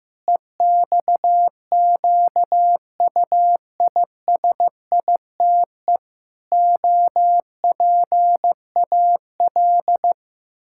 Morse code, heard as a keyed tone, about 15 wpm, 700 Hz. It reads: EXQUISITE OPAL
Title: Pangram Challenge v2 - 15wpm